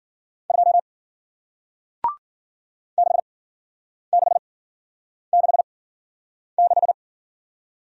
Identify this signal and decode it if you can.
Morse code, heard as a keyed tone, about 53 wpm, spaced as for 9 wpm, 700 Hz. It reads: 2 6666